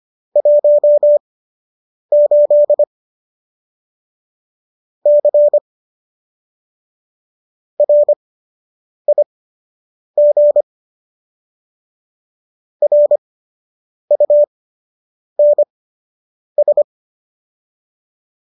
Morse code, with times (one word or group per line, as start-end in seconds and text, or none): 0.4-2.8 18
5.1-5.6 C
7.8-10.6 RIG
12.8-16.8 RUNS